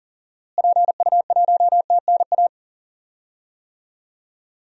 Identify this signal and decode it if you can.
Morse code, heard as a keyed tone, about 40 wpm, 700 Hz. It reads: PU1TNA
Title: DXCC-100 Call Signs - 6 Characters - Repeated 2x - 40wpm